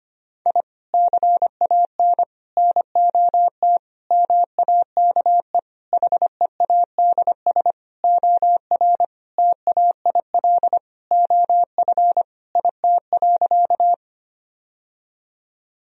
Morse code, heard as a keyed tone, about 25 wpm, 700 Hz. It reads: I CAN NOT MAKE HEADS OR TAIL OF IT.
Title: Sentences from Top 1000 Words - Review Part 3 - 25wpm